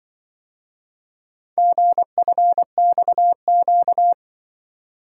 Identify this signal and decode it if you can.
Morse code, heard as a keyed tone, about 25 wpm, 700 Hz. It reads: GFXQ
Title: Speed-Racing - Four Letter-Number Combinations - 20wpm